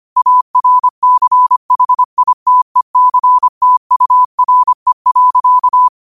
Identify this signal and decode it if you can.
Morse code, heard as a keyed tone, about 25 wpm, 1000 Hz. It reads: ARCHITECTURE.